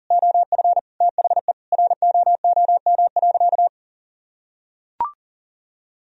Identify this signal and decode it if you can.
Morse code, heard as a keyed tone, about 40 wpm, 700 Hz. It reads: OF THE ROOM.
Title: Sentences from Top 500 Words and Abbreviations 40wpm